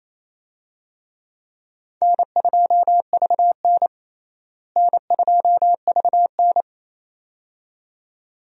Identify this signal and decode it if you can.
Morse code, heard as a keyed tone, about 28 wpm, 700 Hz. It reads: N2VN N2VN
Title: USA Call Signs - 4 Characters - Repeated 2x - 28wpm